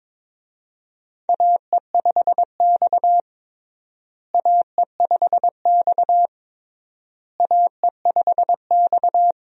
Morse code, heard as a keyed tone, about 22 wpm, 700 Hz. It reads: AE5X AE5X AE5X
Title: USA Call Signs - 4 Characters - Repeated 3x - 22wpm